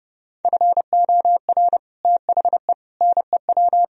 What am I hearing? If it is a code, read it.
Morse code, heard as a keyed tone, about 29 wpm, 700 Hz. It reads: FOR THE NEW